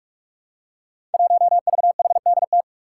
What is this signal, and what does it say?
Morse code, heard as a keyed tone, about 46 wpm, 700 Hz. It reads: 1VHDT
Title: Five Challenging Letter-Number Combinations 45wpm